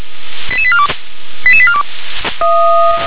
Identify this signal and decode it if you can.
Five-tone selective call (ZVEI), heard as quick runs of five tones, tones of about 70 ms. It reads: 88842 88842